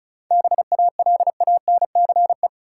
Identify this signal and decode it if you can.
Morse code, heard as a keyed tone, about 36 wpm, 700 Hz. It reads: BALANCE
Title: Sentences from Top 900 Words - New Word Drill - 35wpm